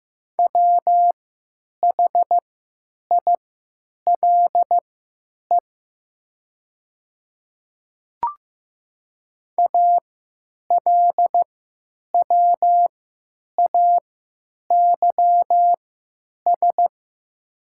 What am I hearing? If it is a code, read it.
Morse code, heard as a keyed tone, about 15 wpm, spaced as for 9 wpm, 700 Hz. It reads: WHILE ALWAYS
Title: 3x Character Spacing - Top 500 Words - 15wpm